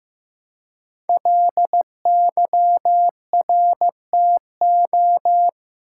Morse code, heard as a keyed tone, about 15 wpm, 700 Hz. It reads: LYRTO
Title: Code Groups - 5 Letters - 15wpm